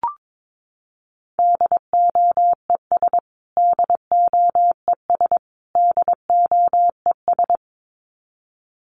Morse code, heard as a keyed tone, about 22 wpm, 700 Hz. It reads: DOES DOES DOES
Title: Top 500 Words Repeated 22wpm